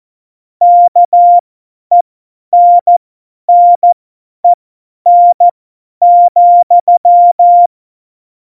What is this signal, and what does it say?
Morse code, heard as a keyed tone, about 14 wpm, spaced as for 10 wpm, 700 Hz. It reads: KENNEN,
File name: Morsekurs / Witze / Witz 15. August 17, Tempo 50 F70